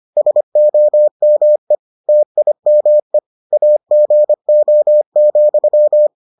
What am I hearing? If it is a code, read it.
Morse code, heard as a keyed tone, about 25 wpm, 600 Hz. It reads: SOME TIME AGO,